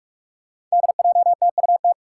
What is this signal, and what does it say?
Morse code, heard as a keyed tone, about 46 wpm, 700 Hz. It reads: DJTUT
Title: Five Challenging Letter-Number Combinations 45wpm